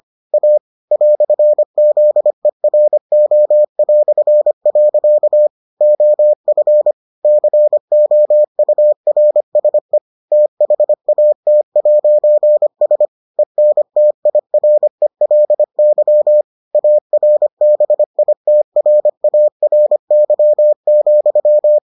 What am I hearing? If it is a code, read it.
Morse code, heard as a keyed tone, about 25 wpm, 600 Hz. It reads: A "ZERO". OF COURSE THAT'S ENTIRELY ARBITRARY,